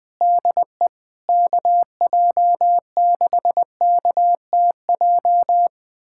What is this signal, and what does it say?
Morse code, heard as a keyed tone, about 20 wpm, 700 Hz. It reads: DE KJ6KTJ
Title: USA Callsign DE Callsign 20wpm